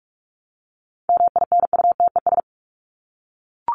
Morse code, heard as a keyed tone, about 44 wpm, 700 Hz. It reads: MINUTES
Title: ES:S Suffix - Top 100 Words with Base Words 45wpm